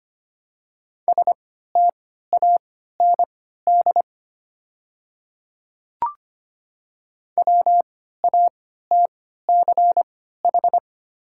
Morse code, heard as a keyed tone, about 25 wpm, spaced as for 14 wpm, 700 Hz. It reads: STAND WATCH
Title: 3x Character Spacing - Top 300 Words - 25wpm